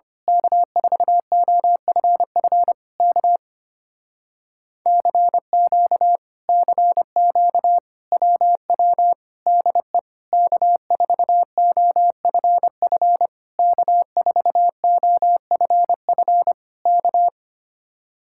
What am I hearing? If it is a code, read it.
Morse code, heard as a keyed tone, about 24 wpm, 700 Hz. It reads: K4OFF K CQ CQ WW DE K4OFF K4OFF K
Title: Speed-Racing Common QSO Phrases Encore - 25wpm